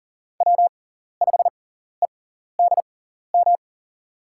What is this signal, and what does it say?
Morse code, heard as a keyed tone, about 39 wpm, spaced as for 14 wpm, 700 Hz. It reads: W5EDM
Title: USA Call Signs - 5 Characters - 6x Character Spacing - 40wpm